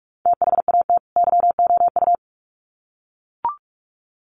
Morse code, heard as a keyed tone, about 44 wpm, 700 Hz. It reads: THAT YOU
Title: Sets of 2 Words 45wpm